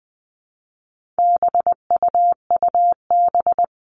Morse code, heard as a keyed tone, about 20 wpm, 700 Hz. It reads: BUUB
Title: All Cut Numbers - 4-Digit 20wpm